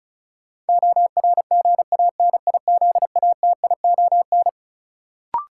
Morse code, heard as a keyed tone, about 34 wpm, 700 Hz. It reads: ORGANIZATION
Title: Sentences from Top 1100 Words - New Word Drill Alt - 35wpm